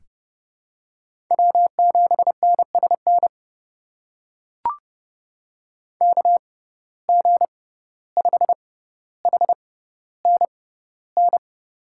Morse code, heard as a keyed tone, about 30 wpm, spaced as for 15 wpm, 700 Hz. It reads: W7NSN KG5HNN
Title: USA Call Signs - 6x Character Spacing - 30wpm